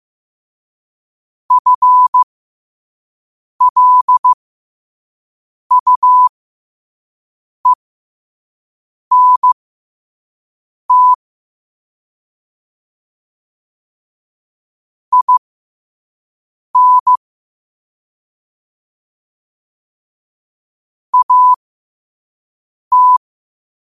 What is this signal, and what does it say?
Morse code, heard as a keyed tone, about 15 wpm, spaced as for 5 wpm, 1000 Hz. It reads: FLUENT IN AT
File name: qotdfiles / 5wpm